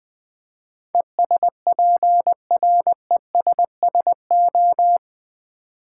Morse code, heard as a keyed tone, about 20 wpm, 690 Hz. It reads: ESPRESSO
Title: Difficult Words with Hints - 20wpm